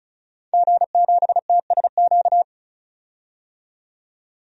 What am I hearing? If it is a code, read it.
Morse code, heard as a keyed tone, about 35 wpm, 700 Hz. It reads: G7TSQ
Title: DXCC-100 Call Signs - 5 Characters - 35wpm